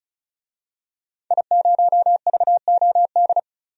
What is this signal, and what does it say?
Morse code, heard as a keyed tone, about 34 wpm, 700 Hz. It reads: I0VOD
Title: DXCC-100 Call Signs - 5 Characters - 35wpm